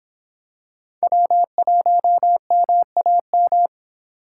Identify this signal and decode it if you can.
Morse code, heard as a keyed tone, about 26 wpm, 700 Hz. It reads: W1MAM